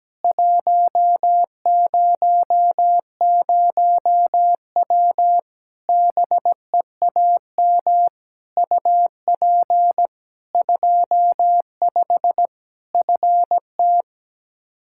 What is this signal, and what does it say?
Morse code, heard as a keyed tone, about 17 wpm, 700 Hz. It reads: 100W BEAM UP 25 FT
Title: Practice QSO - Exchange 2 - Info - 17wpm